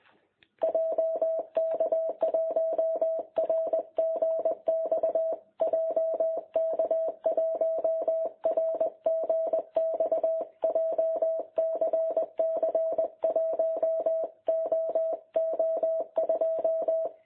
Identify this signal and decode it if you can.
Morse code, heard as a keyed tone, about 22 wpm, 660 Hz. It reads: JK1RGX JK1RGX JCC 1OO2
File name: jh8cbh / 2021cw